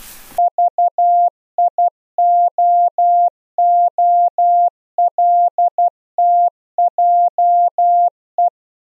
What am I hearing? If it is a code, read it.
Morse code, heard as a keyed tone, about 12 wpm, 700 Hz. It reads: VIOOLTJE
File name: Zendamateurnieuws en een stukje geseinde morse